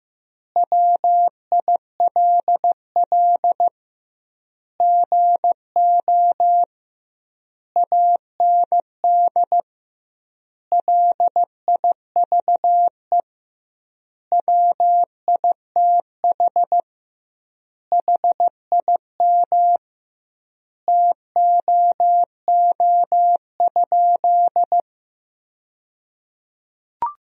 Morse code, heard as a keyed tone, about 16 wpm, 700 Hz. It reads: WILL GO AND LIVE WITH HIM TOO?
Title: 2x Word-Spacing - Sentences from Top 500 Words - 15wpm